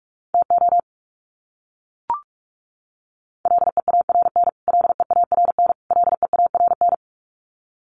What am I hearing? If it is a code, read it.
Morse code, heard as a keyed tone, about 44 wpm, 700 Hz. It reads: TO LEARN LEARN LEARN
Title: Top 500 Words Repeated 45wpm